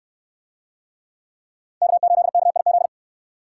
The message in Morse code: K8KIZ

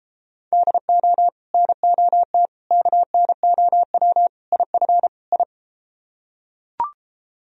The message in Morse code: DO NOT KNOW IF I

